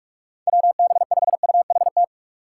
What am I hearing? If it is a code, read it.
Morse code, heard as a keyed tone, about 42 wpm, 700 Hz. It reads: WB5UHT
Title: USA Call Signs - 6 Characters - 3x Character Spacing - 45wpm